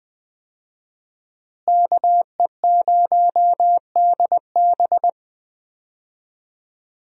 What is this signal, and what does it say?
Morse code, heard as a keyed tone, about 20 wpm, 700 Hz. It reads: KE0DB
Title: USA Call Signs - 5 Characters - Repeated 3x - 20wpm